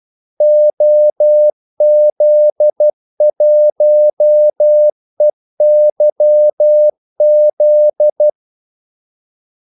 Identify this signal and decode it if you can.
Morse code, heard as a keyed tone, about 12 wpm, 600 Hz. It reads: OZ1EYZ
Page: 2 ekstra ordmellemrum mellem kaldesignalerne